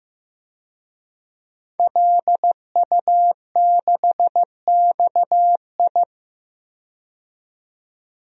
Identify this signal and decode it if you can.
Morse code, heard as a keyed tone, about 15 wpm, 700 Hz. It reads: LU6XI
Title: DXCC-100 Call Signs - 5 Characters - 15wpm